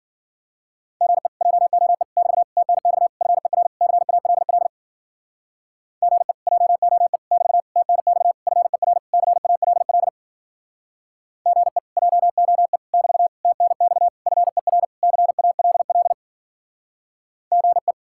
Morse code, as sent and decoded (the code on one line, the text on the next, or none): --. / .--- --- . / -...- / - -. -..- / ..-. .-. / -.-. .- .-.. .-.. / --. / .--- --- . / -...- / - -. -..- / ..-. .-. / -.-. .- .-.. .-.. / --. / .--- --- . / -...- / - -. -..- / ..-. .-. / -.-. .- .-.. .-.. / --. .
GE JOE <BT> TNX FER CALL GE JOE <BT> TNX FER CALL GE JOE <BT> TNX FER CALL GE